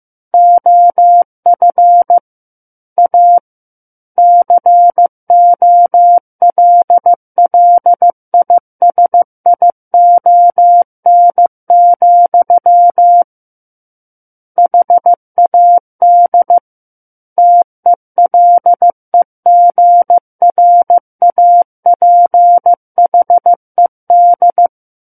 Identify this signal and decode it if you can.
Morse code, heard as a keyed tone, about 15 wpm, 700 Hz. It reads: OF A COLLISION, HAD TELEGRAPHED
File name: Book One / 06-Chapter 6 - 15 WPM